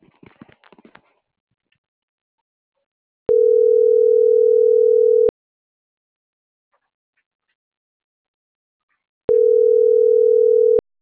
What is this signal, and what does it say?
Ringback tone (North America), 440 + 480 Hz.